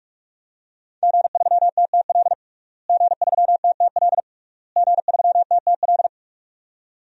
Morse code, heard as a keyed tone, about 46 wpm, 700 Hz. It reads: G3TTL G3TTL G3TTL
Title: DXCC-100 Call Signs - 5 Characters - Repeated 3x - 45wpm